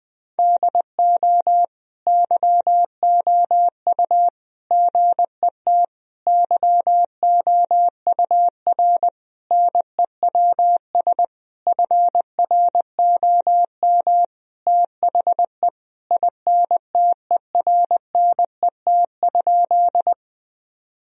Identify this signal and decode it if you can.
Morse code, heard as a keyed tone, about 20 wpm, 700 Hz. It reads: DO YOU GET YOUR NEWS FROM THE INTERNET?
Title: Sentences from Top 1000 Words - Review Part 4 - 20wpm